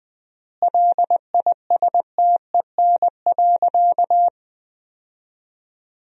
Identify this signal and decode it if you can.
Morse code, heard as a keyed tone, about 21 wpm, 700 Hz. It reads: LISTEN.
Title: 4x Word-Spacing - Sentences from Top 500 Words - 20wpm